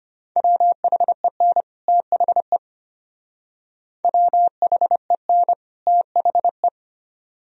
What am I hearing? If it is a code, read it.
Morse code, heard as a keyed tone, about 32 wpm, 700 Hz. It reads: WHEN THE WHEN THE